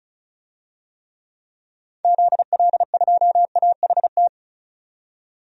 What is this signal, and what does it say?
Morse code, heard as a keyed tone, about 36 wpm, 700 Hz. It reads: ZL2AHT
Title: DXCC-100 Call Signs - 6 Characters - 35wpm